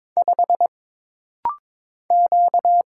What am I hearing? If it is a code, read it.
Morse code, heard as a keyed tone, about 22 wpm, 700 Hz. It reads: 5 Q
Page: This rapid-fire timing is geared to push your ICR (Instant Character Recognition) skills.